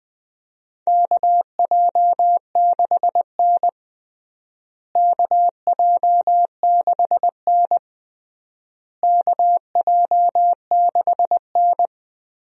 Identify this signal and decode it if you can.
Morse code, heard as a keyed tone, about 20 wpm, 700 Hz. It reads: KJ6N KJ6N KJ6N
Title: USA Call Signs - 4 Characters - Repeated 3x - 20wpm